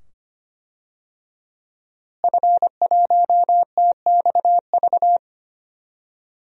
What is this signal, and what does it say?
Morse code, heard as a keyed tone, about 25 wpm, 700 Hz. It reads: F1TXV
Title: DXCC-100 Call Signs - 5 Characters - Repeated 2x - 25wpm